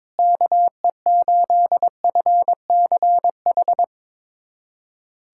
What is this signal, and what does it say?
Morse code, heard as a keyed tone, about 22 wpm, 700 Hz. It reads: KE8FCH
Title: USA Callsign DE Callsign - 6 Characters - 22wpm